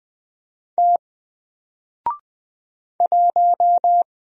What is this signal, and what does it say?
Morse code, heard as a keyed tone, about 20 wpm, 700 Hz. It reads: T 1
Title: Single Letter-Number Alt 20wpm